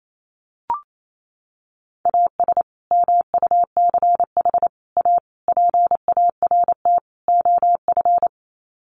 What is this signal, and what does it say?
Morse code, heard as a keyed tone, about 28 wpm, 700 Hz. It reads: AS MUCH A PART OF
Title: Sets of 5 Words Encore2 28wpm